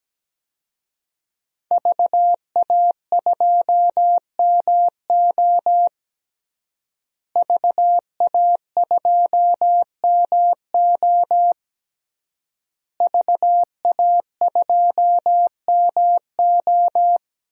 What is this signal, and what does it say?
Morse code, heard as a keyed tone, about 17 wpm, 700 Hz. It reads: VA2MO VA2MO VA2MO
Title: DXCC-100 Call Signs - 5 Characters - Repeated 3x - 17wpm